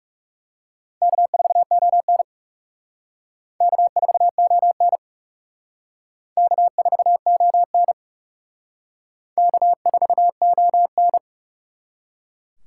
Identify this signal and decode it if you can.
Morse code, heard as a keyed tone, about 37 wpm, 700 Hz. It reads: K4ON K4ON K4ON K4ON
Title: Speed Racing - USA 1x2 Call Signs 30wpm